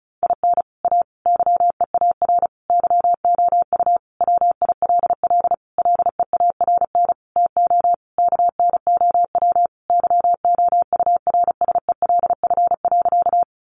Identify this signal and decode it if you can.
Morse code, heard as a keyed tone, about 34 wpm, 700 Hz. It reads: IN A YEAR YOU WILL LEARN TO KNOW YOURSELF.
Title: Sentences from Top 600 Words 35wpm